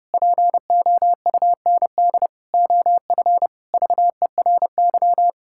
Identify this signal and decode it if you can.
Morse code, heard as a keyed tone, about 28 wpm, 700 Hz. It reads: POUND OF VERY
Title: Pangram Challenge v2 - 30wpm